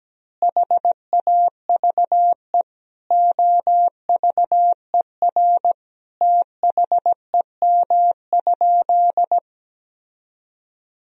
Morse code, heard as a keyed tone, about 17 wpm, 700 Hz. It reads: HAVE OVER THEM?
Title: Sentences from Top 1000 Words - Review Part 3 - 17wpm